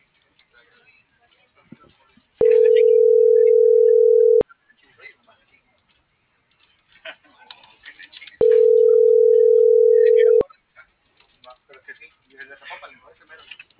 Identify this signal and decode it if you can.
Ringback tone (North America), 440 + 480 Hz, with speech.
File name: CENTRAL AGUILERA